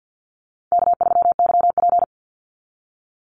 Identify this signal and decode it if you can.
Morse code, heard as a keyed tone, about 50 wpm, 700 Hz. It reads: K3YP
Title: Easy Call Signs - 50wpm